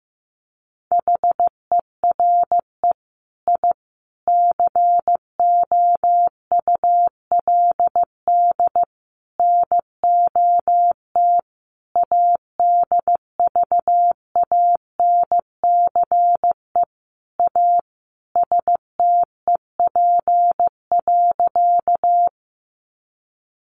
Morse code, heard as a keyed tone, about 15 wpm, 700 Hz. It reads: HERE I COULD NOT ADVANCE A STEP.
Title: Sentences from Top 1400 Words 15wpm